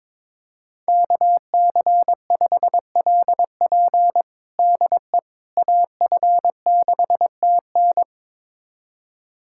Morse code, heard as a keyed tone, about 22 wpm, 700 Hz. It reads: KC5LP DE AF6TN
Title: USA Callsign DE Callsign - 5 Characters - 22wpm